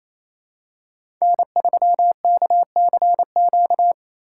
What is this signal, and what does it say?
Morse code, heard as a keyed tone, about 28 wpm, 700 Hz. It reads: N3KCQ
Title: USA Call Signs - 5 Characters - 4x Character Spacing - 28wpm